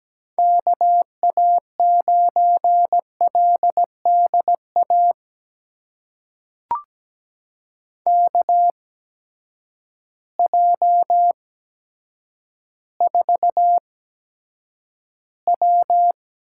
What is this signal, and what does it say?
Morse code, heard as a keyed tone, about 17 wpm, spaced as for 9 wpm, 700 Hz. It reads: KA9LDA KJ4W
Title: USA Call Signs - 6 Characters - 8x Character Spacing - 17wpm